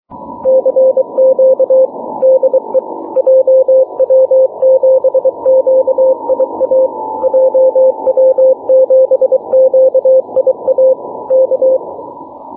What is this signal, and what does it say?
Morse code, heard as a keyed tone, about 23 wpm, 520 Hz. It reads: CQ DE JW7QIA JW7QIA K
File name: wp / jw7qia 6m cw 27 jun 2011